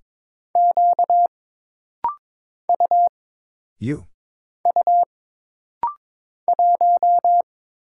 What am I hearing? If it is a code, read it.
Morse code, heard as a keyed tone, about 22 wpm, 700 Hz. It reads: Q U U 1
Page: This rapid-fire timing is geared to push your ICR (Instant Character Recognition) skills.